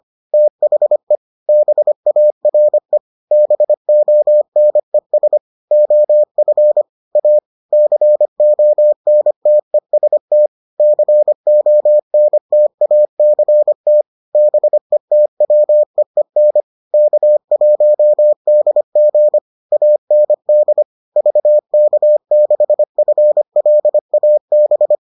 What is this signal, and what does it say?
Morse code, heard as a keyed tone, about 25 wpm, 600 Hz. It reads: THE BARE BONES OF A CONTEST CONTACT BETWEEN K1DG AND VK6FLAB